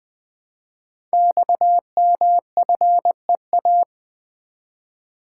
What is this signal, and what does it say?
Morse code, heard as a keyed tone, about 20 wpm, 700 Hz. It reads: XMFEA
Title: Code Groups - 5 Letter:Numbers - 20wpm